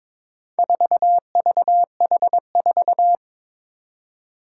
Morse code, heard as a keyed tone, about 21 wpm, 700 Hz. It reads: 4VH4
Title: Four Challenging Letter-Number Combinations 22wpm